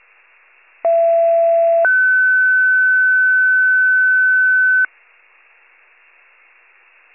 Two-tone page, heard as a one-second tone, then a three-second tone, about 670 then 1500 Hz. It is heard from 0.8 to 4.9 s.
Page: 39 FIRE